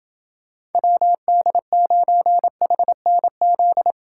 Morse code, heard as a keyed tone, about 30 wpm, 700 Hz. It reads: WD9HNZ